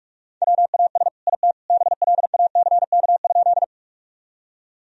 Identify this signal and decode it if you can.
Morse code, heard as a keyed tone, about 46 wpm, 700 Hz. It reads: WAS IT BLACK?
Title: Sentences from Top 1000 Words - Review Part 2 - 45wpm